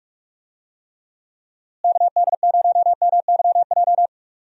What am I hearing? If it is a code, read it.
Morse code, heard as a keyed tone, about 46 wpm, 700 Hz. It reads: KD0MYJ
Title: USA Call Signs - 6 Characters - Repeated 2x - 45wpm